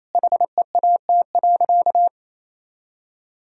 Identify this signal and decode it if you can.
Morse code, heard as a keyed tone, about 28 wpm, 700 Hz. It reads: HEAT.
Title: Sentences from Top 1000 Words - Review Part 3 - 28wpm